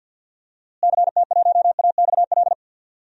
Morse code, heard as a keyed tone, about 52 wpm, 700 Hz. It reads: CT1AXL
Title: DXCC-100 Call Signs - 6 Characters - Repeated 3x - 50wpm